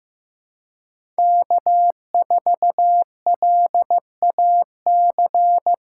Morse code, heard as a keyed tone, about 17 wpm, 700 Hz. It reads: K4LAC
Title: Speed Racing - USA Call Signs 15wpm